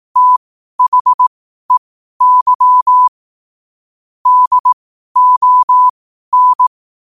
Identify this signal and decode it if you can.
Morse code, heard as a keyed tone, about 18 wpm, spaced as for 12 wpm, 1000 Hz. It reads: THEY DON
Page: Quotes for Wed, 13 Aug 2025 in Morse Code at 12 words per minute.